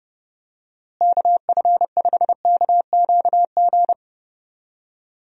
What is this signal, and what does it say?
Morse code, heard as a keyed tone, about 32 wpm, 700 Hz. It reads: KF5KQG
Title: USA Call Signs - 6 Characters - Repeated 2x - 30wpm